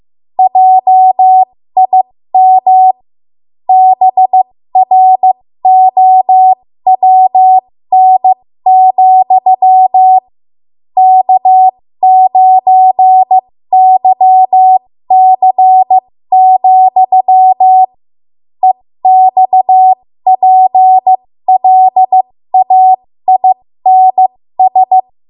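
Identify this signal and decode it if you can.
Morse code, heard as a keyed tone, about 15 wpm, spaced as for 13 wpm, 750 Hz. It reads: JIM BROWN, K9YC, EXPLAINS
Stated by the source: 13 wpm Morse code practice archive transmission.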